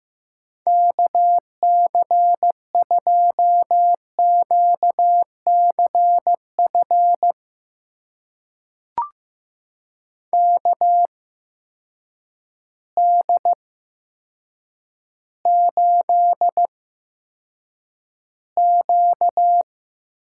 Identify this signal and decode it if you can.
Morse code, heard as a keyed tone, about 15 wpm, spaced as for 8 wpm, 700 Hz. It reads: KC2QCF KD8Q
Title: USA Call Signs - 6 Characters - 8x Character Spacing - 15wpm